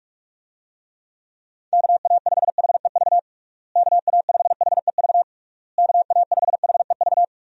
Morse code, heard as a keyed tone, about 46 wpm, 700 Hz. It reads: KA5HEV KA5HEV KA5HEV
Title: USA Call Signs - 6 Characters - Repeated 3x - 45wpm